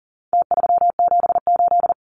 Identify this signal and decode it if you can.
Morse code, heard as a keyed tone, about 40 wpm, 700 Hz. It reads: T378
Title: Numbers - Contest Style - 4-Digit 40wpm